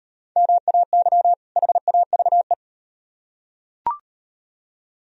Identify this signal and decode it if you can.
Morse code, heard as a keyed tone, about 38 wpm, 700 Hz. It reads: MAY HAVE